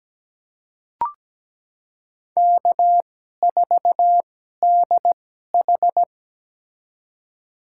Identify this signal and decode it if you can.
Morse code, heard as a keyed tone, about 17 wpm, spaced as for 12 wpm, 700 Hz. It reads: K4DH
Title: USA Call Signs - 4 Characters - 2x Character Spacing - 17wpm